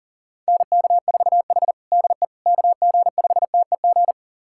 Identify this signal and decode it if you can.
Morse code, heard as a keyed tone, about 40 wpm, 700 Hz. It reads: NK4H DE KG5TEG
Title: USA Callsign DE Callsign 40wpm